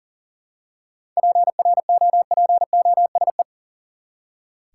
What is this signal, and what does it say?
Morse code, heard as a keyed tone, about 40 wpm, 700 Hz. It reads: PROPOSE
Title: Seven Letter Words - 40wpm